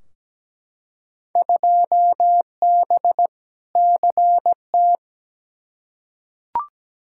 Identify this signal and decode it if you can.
Morse code, heard as a keyed tone, about 17 wpm, 700 Hz. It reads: 2B CT
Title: Field Day Exchanges Encore - 17wpm